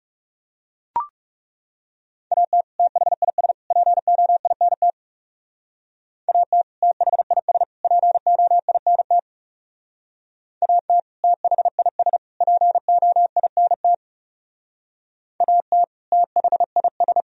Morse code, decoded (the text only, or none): AT THIS POINT AT THIS POINT AT THIS POINT AT THIS